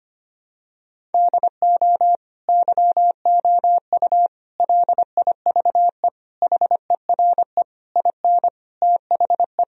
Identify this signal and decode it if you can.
Morse code, heard as a keyed tone, about 25 wpm, 700 Hz. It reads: DO YOU LIVE HERE IN THE
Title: Sentences from Top 1000 Words - Review Part 1 - 25wpm